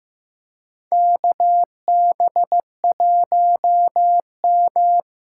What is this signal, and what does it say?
Morse code, heard as a keyed tone, about 15 wpm, 700 Hz. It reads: KB1M
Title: USA Call Signs - 6 Characters - 6x Character Spacing - 15wpm